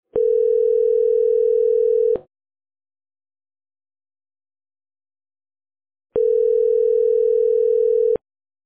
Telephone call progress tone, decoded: ringback tone